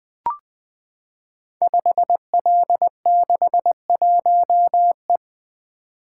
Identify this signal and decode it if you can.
Morse code, heard as a keyed tone, about 20 wpm, 700 Hz. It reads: 5L61E
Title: Code Groups - 5 Letter:Numbers - 20wpm